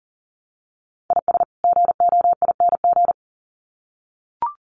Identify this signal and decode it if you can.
Morse code, heard as a keyed tone, about 40 wpm, 700 Hz. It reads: IS GOING
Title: Sets of 2 Words 40wpm